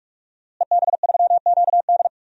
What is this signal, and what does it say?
Morse code, heard as a keyed tone, about 44 wpm, 700 Hz. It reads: EB3QD
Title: DXCC-100 Call Signs - 5 Characters - Repeated 2x - 45wpm